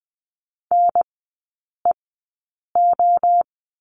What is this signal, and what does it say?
Morse code, heard as a keyed tone, about 20 wpm, spaced as for 8 wpm, 700 Hz. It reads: NEO
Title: 035 Lesson 7e - Letter s - Three Characters 20wpm